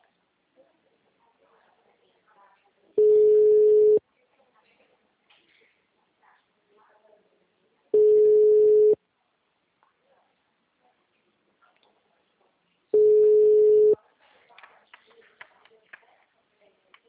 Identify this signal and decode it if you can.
Ringback tone (Europe, 425 Hz), with speech.
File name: out- -2012- - - .506